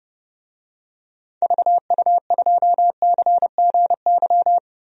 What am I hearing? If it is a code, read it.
Morse code, heard as a keyed tone, about 30 wpm, 700 Hz. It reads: VU2CGY
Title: DXCC-100 Call Signs - 6 Characters - 30wpm